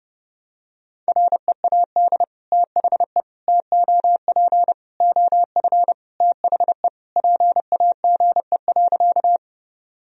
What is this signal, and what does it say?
Morse code, heard as a keyed tone, about 30 wpm, 700 Hz. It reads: READ THE TOP OF THE PAGE.
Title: Sentences from Top 1000 Words - Review Part 2 - 30wpm